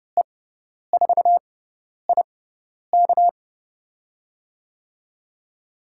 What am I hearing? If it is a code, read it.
Morse code, heard as a keyed tone, about 30 wpm, spaced as for 10 wpm, 700 Hz. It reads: E4IK
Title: USA Call Signs - 5 Characters - 6x Character Spacing - 30wpm